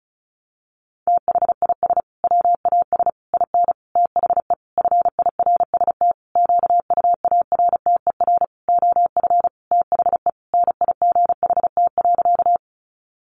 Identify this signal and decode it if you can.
Morse code, heard as a keyed tone, about 34 wpm, 700 Hz. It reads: THIS WAS IN THE FIRST QUARTER OF THE NIGHT.
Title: Sentences from Top 1300 Words 35wpm